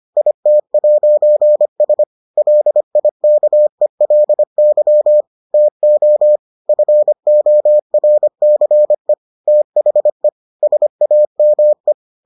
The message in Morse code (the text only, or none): IT'S LIKELY TO FORCE THE SAME